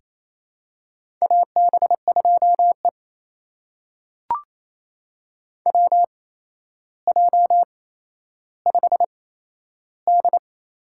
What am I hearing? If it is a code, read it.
Morse code, heard as a keyed tone, about 28 wpm, spaced as for 12 wpm, 700 Hz. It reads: AB2E WJ5D
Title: USA Call Signs - 4 Characters - 8x Character Spacing - 28wpm